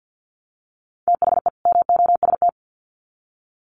Morse code, heard as a keyed tone, about 50 wpm, 700 Hz. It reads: THE MOST
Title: Sets of 2 Words 50wpm